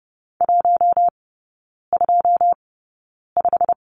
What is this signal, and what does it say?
Morse code, heard as a keyed tone, about 30 wpm, spaced as for 9 wpm, 700 Hz. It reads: 125
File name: course / 30:10wpm / 087 Lesson 15e - Number 5 - Three Characters 30wpm